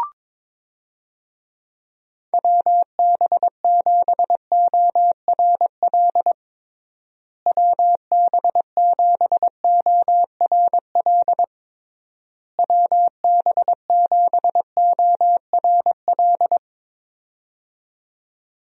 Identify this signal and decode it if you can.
Morse code, heard as a keyed tone, about 21 wpm, 700 Hz. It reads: WB7ORL WB7ORL WB7ORL
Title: USA Call Signs - 6 Characters - Repeated 3x - 22wpm